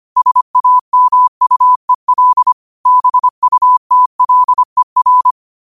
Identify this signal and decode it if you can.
Morse code, heard as a keyed tone, about 25 wpm, 1000 Hz. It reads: SAMUEL BUTLER